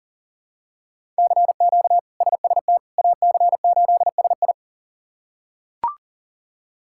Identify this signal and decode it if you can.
Morse code, heard as a keyed tone, about 40 wpm, 700 Hz. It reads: CQ SST AC8SI
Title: K1USN SST QSO Practice - 40wpm